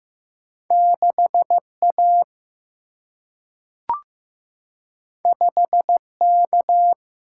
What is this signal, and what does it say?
Morse code, heard as a keyed tone, about 15 wpm, 700 Hz. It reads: 6 A 5 K
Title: Single-Letter Single-Number 15wpm